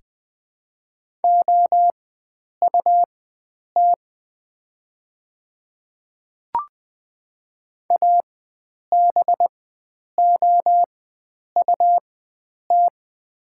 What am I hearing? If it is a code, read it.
Morse code, heard as a keyed tone, about 20 wpm, spaced as for 9 wpm, 700 Hz. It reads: OUT ABOUT
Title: 4x Character Spacing - Top 100 Words - 20wpm